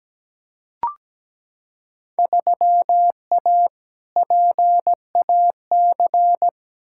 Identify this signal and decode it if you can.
Morse code, heard as a keyed tone, about 17 wpm, 700 Hz. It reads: 3A PAC
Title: Field Day Exchanges - 17wpm